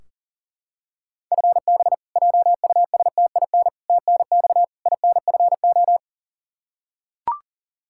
Morse code, heard as a keyed tone, about 40 wpm, 700 Hz. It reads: FB JUSTIN TNX INFO
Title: Practice QSO - Exchange 3 - More Info - 40wpm